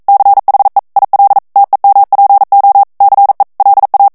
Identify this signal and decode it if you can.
Morse code, heard as a keyed tone, about 42 wpm, 800 Hz. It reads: CHE IL TEMPO CE LA
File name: morse mp3 / ITALIEN texte clair 1 vitesse 40 mots minute